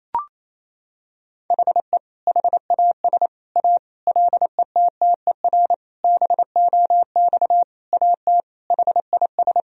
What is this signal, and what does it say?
Morse code, heard as a keyed tone, about 28 wpm, 700 Hz. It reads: HE HAS A LETTER BOX AT HIS
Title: Sentences from Top 1000 Words - Review Part 2 - 28wpm